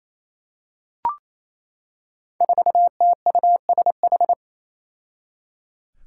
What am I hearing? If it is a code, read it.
Morse code, heard as a keyed tone, about 28 wpm, 700 Hz. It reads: 4TUSH
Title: Five Challenging Letter-Number Combinations 28wpm